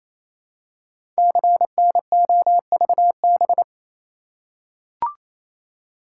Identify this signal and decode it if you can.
Morse code, heard as a keyed tone, about 29 wpm, 700 Hz. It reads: CNOVB
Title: Code Groups - 5 Letters - 28wpm